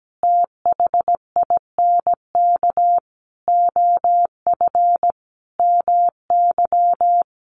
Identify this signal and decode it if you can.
Morse code, heard as a keyed tone, about 17 wpm, 700 Hz. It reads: THINK OF MY
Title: Sentences from Top 1400 Words 17wpm